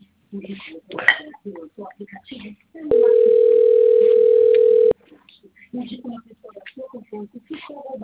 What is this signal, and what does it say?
Ringback tone (North America), 440 + 480 Hz, with speech.
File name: rg-1600-unknown- - - .56910